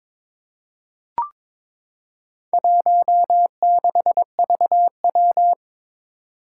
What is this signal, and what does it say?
Morse code, heard as a keyed tone, about 22 wpm, 700 Hz. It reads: 16VW
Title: Four Challenging Letter-Number Combinations 22wpm